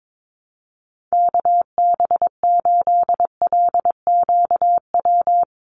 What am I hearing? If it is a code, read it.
Morse code, heard as a keyed tone, about 22 wpm, 700 Hz. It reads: KB8LQW
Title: USA Call Signs 22wpm